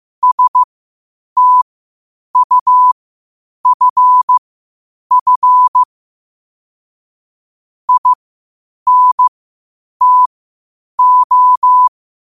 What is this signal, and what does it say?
Morse code, heard as a keyed tone, about 15 wpm, spaced as for 8 wpm, 1000 Hz. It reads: STUFF INTO
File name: qotdfiles / 8wpm